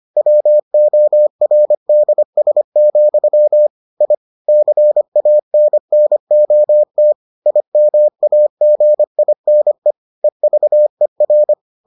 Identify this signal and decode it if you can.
Morse code, heard as a keyed tone, about 25 wpm, 600 Hz. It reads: WORDS, I CANNOT IMAGINE EVER